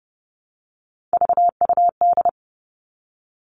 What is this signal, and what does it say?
Morse code, heard as a keyed tone, about 30 wpm, 700 Hz. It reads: VUD
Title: Three Challenging Letter-Number Combinations 30wpm